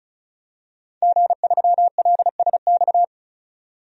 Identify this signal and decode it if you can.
Morse code, heard as a keyed tone, about 34 wpm, 700 Hz. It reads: G3LSX
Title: DXCC-100 Call Signs - 5 Characters - 35wpm